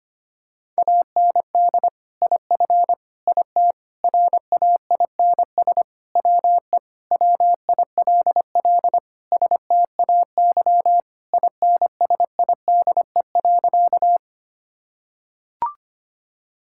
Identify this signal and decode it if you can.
Morse code, heard as a keyed tone, about 25 wpm, 700 Hz. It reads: AND IF IT RAINS WE WILL STAY INSIDE.
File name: practice / 25wpm / Sentences with 2-Word Phrases - 25wpm